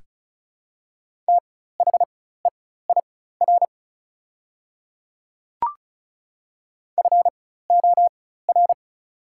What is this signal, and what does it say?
Morse code, heard as a keyed tone, about 36 wpm, spaced as for 16 wpm, 690 Hz. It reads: THEIR FOR
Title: 4x Character Spacing - Top 200 Words - 35wpm